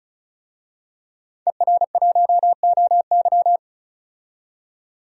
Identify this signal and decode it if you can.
Morse code, heard as a keyed tone, about 34 wpm, 700 Hz. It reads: ER1OY
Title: DXCC-100 Call Signs - 5 Characters - 35wpm